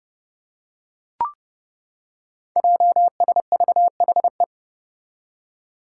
Morse code, heard as a keyed tone, about 30 wpm, 700 Hz. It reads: JSVHE
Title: Five Challenging Letter-Number Combinations 30wpm